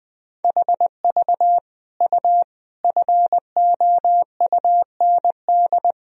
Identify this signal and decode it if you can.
Morse code, heard as a keyed tone, about 21 wpm, 700 Hz. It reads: HV U FOUND
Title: Sentences from Top 300 Words and Abbreviations 20wpm